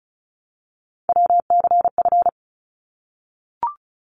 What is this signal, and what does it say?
Morse code, heard as a keyed tone, about 34 wpm, 700 Hz. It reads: WCF
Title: ARRL Sections 35wpm